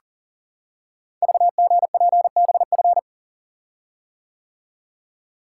Morse code, heard as a keyed tone, about 42 wpm, 700 Hz. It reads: VGPBF
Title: Code Groups - 5 Letters - 40wpm